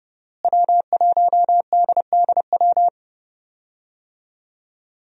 Morse code, heard as a keyed tone, about 30 wpm, 700 Hz. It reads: W1DDW
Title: USA Call Signs - 5 Characters - Repeated 2x - 30wpm